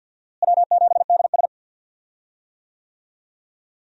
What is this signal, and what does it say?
Morse code, heard as a keyed tone, about 51 wpm, 700 Hz. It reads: W7DS
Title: USA Call Signs - 4 Characters - Repeated 3x - 50wpm